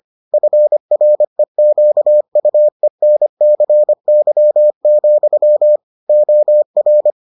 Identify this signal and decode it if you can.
Morse code, heard as a keyed tone, about 25 wpm, 600 Hz. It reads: FREQUENCY, OR